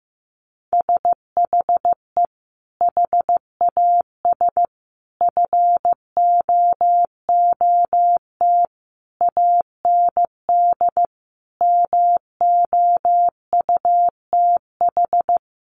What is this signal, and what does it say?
Morse code, heard as a keyed tone, about 15 wpm, 700 Hz. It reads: SHE HAS FOOT AND MOUTH
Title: Sentences from Top 1500 Words 15wpm